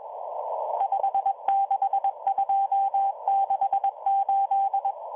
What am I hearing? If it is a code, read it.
Morse code, heard as a keyed tone, about 22 wpm, 790 Hz, with radio noise. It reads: 56268